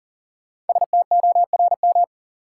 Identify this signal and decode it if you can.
Morse code, heard as a keyed tone, about 40 wpm, 700 Hz. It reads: STORM
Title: Sentences from Top 1500 Words - New Word Drill - 40wpm